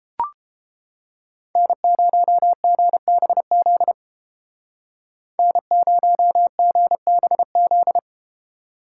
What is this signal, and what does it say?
Morse code, heard as a keyed tone, about 33 wpm, 700 Hz. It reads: N0GBZ N0GBZ